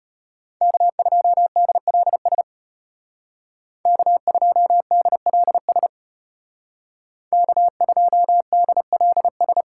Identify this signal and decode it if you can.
Morse code, heard as a keyed tone, about 30 wpm, 700 Hz. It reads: K2DLS K2DLS K2DLS
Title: Speed Racing - USA 1x3 Call Signs 25wpm